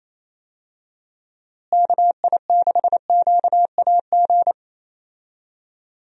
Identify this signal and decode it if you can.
Morse code, heard as a keyed tone, about 27 wpm, 700 Hz. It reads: KI6QAG
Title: USA Call Signs - 6 Characters - Repeated 3x - 28wpm